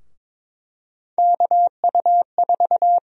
Morse code, heard as a keyed tone, about 22 wpm, 700 Hz. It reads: KU4
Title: USA Call Sign Prefixes - Repeated 2x - 22wpm